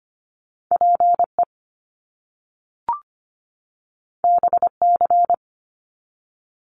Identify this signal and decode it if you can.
Morse code, heard as a keyed tone, about 25 wpm, 700 Hz. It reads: PE BC
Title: Canadian Provinces and Territories 25wpm